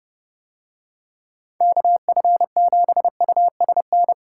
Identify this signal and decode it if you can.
Morse code, heard as a keyed tone, about 28 wpm, 700 Hz. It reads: KF7USN